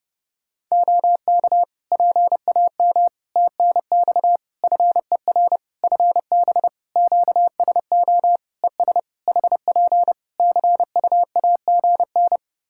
Morse code, heard as a keyed tone, about 28 wpm, 700 Hz. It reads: OK PAM TNX FER FB QSO ES HP CUAGN
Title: K1USN SST QSO Practice 30wpm